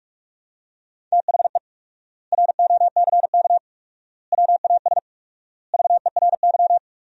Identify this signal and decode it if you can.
Morse code, heard as a keyed tone, about 45 wpm, 700 Hz. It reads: THE ROCK WAS VERY
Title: 4x Word-Spacing - Sentences from Top 400 Words - 45wpm